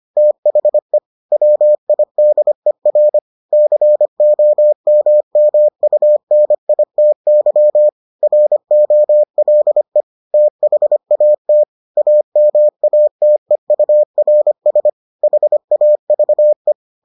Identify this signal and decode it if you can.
Morse code, heard as a keyed tone, about 25 wpm, 600 Hz. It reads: THE WIDER COMMUNITY ROLE THAT AMATEURS HAVE